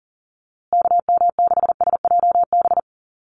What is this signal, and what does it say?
Morse code, heard as a keyed tone, about 40 wpm, 700 Hz. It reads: KM6SJB
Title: USA Call Signs Repeated 3x 40wpm